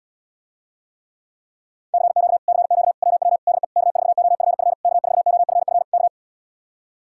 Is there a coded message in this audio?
Yes: Morse code from 1.9 to 6.1 s, heard as a keyed tone, about 80 wpm, 700 Hz. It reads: CQ CQ WW DE K4OFF K4OFF K